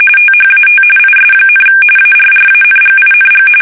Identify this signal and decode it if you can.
RTTY, about 45.45 baud. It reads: THAT IS, ARRANGING
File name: fsk850